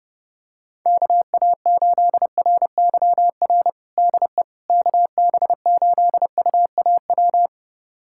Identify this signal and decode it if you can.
Morse code, heard as a keyed tone, about 30 wpm, 700 Hz. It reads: KA8RYR DE KB8UAW
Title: USA Callsign DE Callsign 30wpm